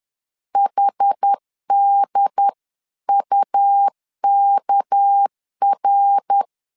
Morse code, heard as a keyed tone, about 10 wpm, 800 Hz. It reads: HDUKR
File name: CW / DET / 1 6 1500C S B 10wpm 4137